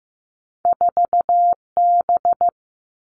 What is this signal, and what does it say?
Morse code, heard as a keyed tone, about 15 wpm, 700 Hz. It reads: 4B
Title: Two Challenging Letter-Number Combinations 15wpm